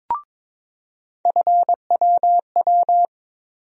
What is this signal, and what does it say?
Morse code, heard as a keyed tone, about 22 wpm, 700 Hz. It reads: FWW
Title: USA Call Sign Postfixes - 3 Letters 22wpm